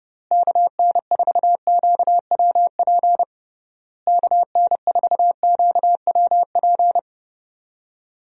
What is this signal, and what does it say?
Morse code, heard as a keyed tone, about 30 wpm, 700 Hz. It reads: KN4QWP KN4QWP
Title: USA Call Signs - 6 Characters - Repeated 3x - 30wpm